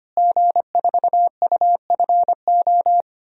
Morse code, heard as a keyed tone, about 25 wpm, 700 Hz. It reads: G4UFO
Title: DXCC-100 Call Signs - 5 Characters - Repeated 2x - 25wpm